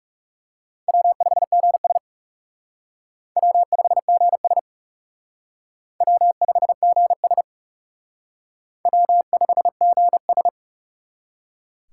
Morse code, heard as a keyed tone, about 40 wpm, 700 Hz. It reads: W5GS W5GS W5GS W5GS